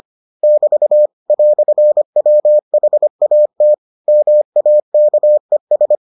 Morse code, heard as a keyed tone, about 25 wpm, 600 Hz. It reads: <BT> "WHAT MAKES